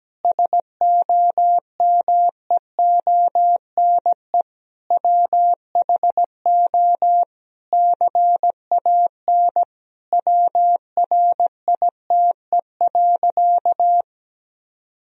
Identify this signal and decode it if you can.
Morse code, heard as a keyed tone, about 17 wpm, 700 Hz. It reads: SOMEONE WHO CAN WRITE.
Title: Sentences from Top 1000 Words - Review Part 1 - 17wpm